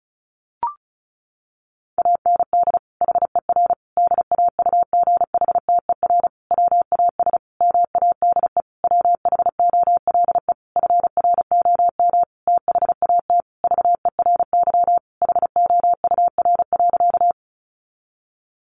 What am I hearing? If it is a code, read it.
Morse code, heard as a keyed tone, about 34 wpm, 700 Hz. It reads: AND HER DAUGHTER WAS MADE WHOLE FROM THAT VERY HOUR.
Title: Sentences from Top 1500 Words 35wpm